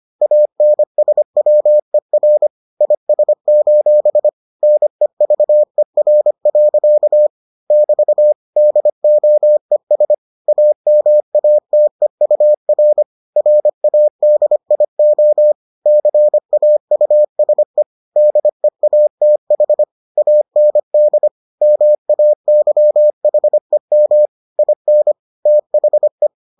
Morse code, heard as a keyed tone, about 25 wpm, 600 Hz. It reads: ANSWER IS: NEVER. <BT> DOES AMATEUR RADIO CAUSE DEATH AND MAYHEM IN THE